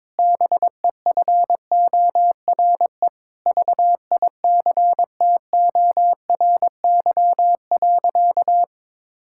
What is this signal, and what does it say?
Morse code, heard as a keyed tone, about 22 wpm, 700 Hz. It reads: BEFORE VICTORY.
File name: Sentences with 9-Word Phrases - 22wpm